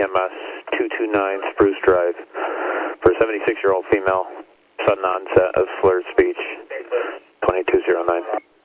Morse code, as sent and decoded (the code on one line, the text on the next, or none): none